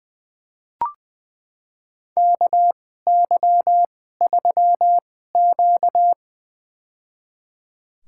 Morse code, heard as a keyed tone, about 20 wpm, spaced as for 15 wpm, 700 Hz. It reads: KY3Q
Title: USA Call Signs - 2x Character Spacing - 20wpm